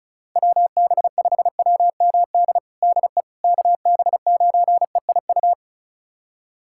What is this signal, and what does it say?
Morse code, heard as a keyed tone, about 36 wpm, 700 Hz. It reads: WB5WMD DE KB9EIU
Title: USA Callsign DE Callsign - 6 Characters - 35wpm